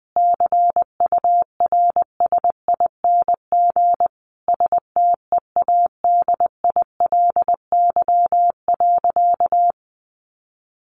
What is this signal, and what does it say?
Morse code, heard as a keyed tone, about 20 wpm, 700 Hz. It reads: CURSING STEADILY.
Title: A Sound Of Thunder 20wpm